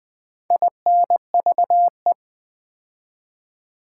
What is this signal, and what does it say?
Morse code, heard as a keyed tone, about 20 wpm, 700 Hz. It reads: INVE
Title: Build A Word - 4 Letters At A Time - 12-Letter Words - 20wpm